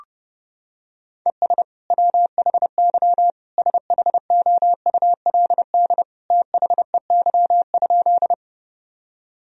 Morse code, heard as a keyed tone, about 30 wpm, 700 Hz. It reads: ES WHY SHOULD THEY?
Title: Sentences from Top 300 Words and Abbreviations 30wpm